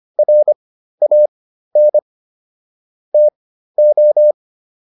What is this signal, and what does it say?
Morse code, heard as a keyed tone, about 25 wpm, spaced as for 13 wpm, 600 Hz. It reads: RAN TO